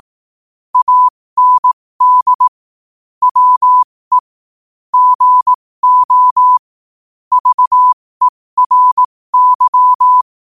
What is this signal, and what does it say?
Morse code, heard as a keyed tone, about 18 wpm, spaced as for 15 wpm, 1000 Hz. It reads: AND WE GO VERY
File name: qotdfiles / 15wpm